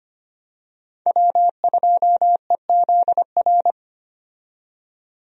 Morse code, heard as a keyed tone, about 25 wpm, 700 Hz. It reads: W2EZR
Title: USA Call Signs - 5 Characters - 25wpm